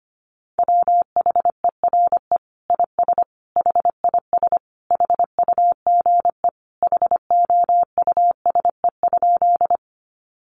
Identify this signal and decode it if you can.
Morse code, heard as a keyed tone, about 25 wpm, 700 Hz. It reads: WHERE IS HIS HUGE HOUSE?
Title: Sentences from Top 1100 Words 25wpm